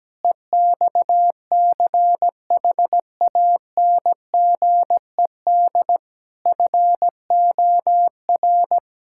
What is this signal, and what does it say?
Morse code, heard as a keyed tone, about 17 wpm, 700 Hz. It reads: EXCHANGED FOR
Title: Pangram Challenge v2 - 17wpm